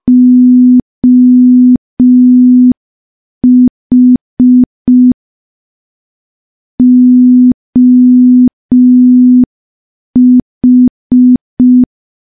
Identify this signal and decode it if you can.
Morse code, heard as a keyed tone, about 5 wpm, 250 Hz. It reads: OH OH
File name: morse / morse-training-05WPM